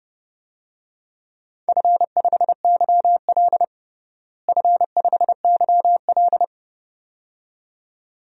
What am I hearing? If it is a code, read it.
Morse code, heard as a keyed tone, about 32 wpm, 700 Hz. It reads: F5YL F5YL
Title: DXCC-100 Call Signs - 4 Characters - Repeated 2x - 30wpm